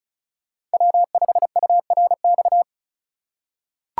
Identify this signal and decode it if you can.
Morse code, heard as a keyed tone, about 36 wpm, 700 Hz. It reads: W5URX
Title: USA Call Signs - 5 Characters - 2x Character Spacing - 35wpm